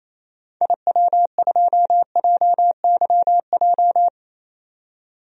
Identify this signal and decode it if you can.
Morse code, heard as a keyed tone, about 28 wpm, 700 Hz. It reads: IW2JYJ
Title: DXCC-100 Call Signs - 6 Characters - 28wpm